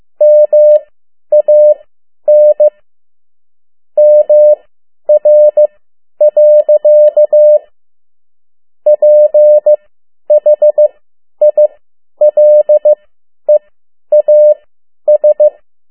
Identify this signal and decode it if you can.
Morse code, heard as a keyed tone, about 15 wpm, spaced as for 10 wpm, 600 Hz. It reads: MAN MR. PHILEAS
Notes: Around the World in 80 Days paragraph one Morse code 10 WPM